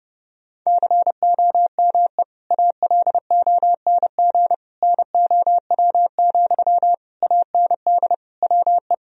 Morse code, heard as a keyed tone, about 32 wpm, 700 Hz. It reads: COME ALONG NOW, AND WE
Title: Sentences from Top 1000 Words - Review Part 4 - 30wpm